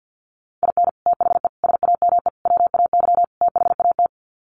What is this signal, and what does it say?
Morse code, heard as a keyed tone, about 50 wpm, 700 Hz. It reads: IN THE SAME WAY THAT
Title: Sets of 5 Words 50wpm